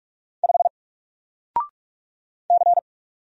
Morse code, heard as a keyed tone, about 44 wpm, 700 Hz. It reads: 5 C
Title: Single Character (Letter:Number:Punctuation:Prosign) 45wpm